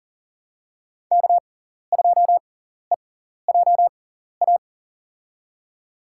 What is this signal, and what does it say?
Morse code, heard as a keyed tone, about 40 wpm, spaced as for 14 wpm, 700 Hz. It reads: K2EJA
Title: USA Call Signs - 5 Characters - 6x Character Spacing - 40wpm